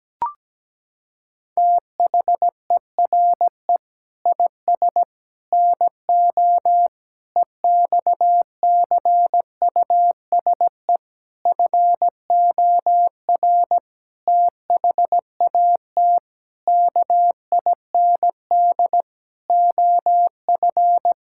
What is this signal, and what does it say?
Morse code, heard as a keyed tone, about 17 wpm, 700 Hz. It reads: THERE IS NO EXCUSE FOR THAT KIND OF
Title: Sentences from Top 1000 Words - Review Part 6 - 17wpm